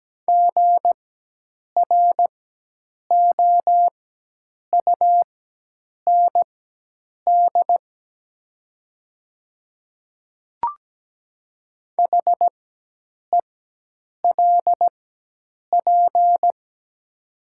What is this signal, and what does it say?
Morse code, heard as a keyed tone, about 17 wpm, spaced as for 8 wpm, 700 Hz. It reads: GROUND HELP
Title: 4x Character Spacing - Top 500 Words - 17wpm